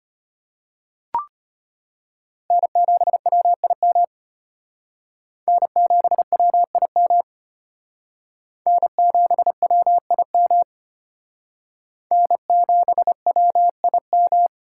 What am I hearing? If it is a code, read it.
Morse code, heard as a keyed tone, about 32 wpm, 700 Hz. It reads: N7WIM N7WIM N7WIM N7WIM